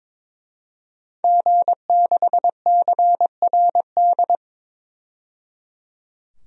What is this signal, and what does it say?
Morse code, heard as a keyed tone, about 22 wpm, 700 Hz. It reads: G6CRD
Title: DXCC-100 Call Signs - 5 Characters - 22wpm